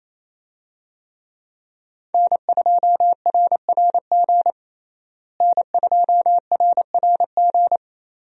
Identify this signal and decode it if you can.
Morse code, heard as a keyed tone, about 28 wpm, 700 Hz. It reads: N2RRG N2RRG
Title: USA Call Signs - 5 Characters - Repeated 3x - 28wpm